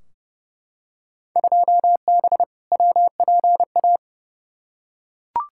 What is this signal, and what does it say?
Morse code, heard as a keyed tone, about 31 wpm, 700 Hz. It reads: 2B WPA
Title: Field Day Exchanges Encore - 30wpm